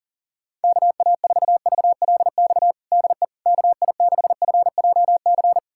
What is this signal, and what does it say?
Morse code, heard as a keyed tone, about 37 wpm, 700 Hz. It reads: KA4VLX DE KI6FJC
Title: USA Callsign DE Callsign - 6 Characters - 40wpm